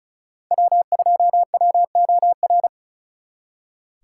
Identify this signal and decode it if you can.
Morse code, heard as a keyed tone, about 42 wpm, 700 Hz. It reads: W2WOR
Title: Speed Racing - USA 1x3 Call Signs 35wpm